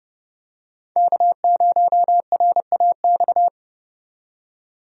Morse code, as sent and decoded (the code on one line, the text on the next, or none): -.- ----- .-. .- -..-
K0RAX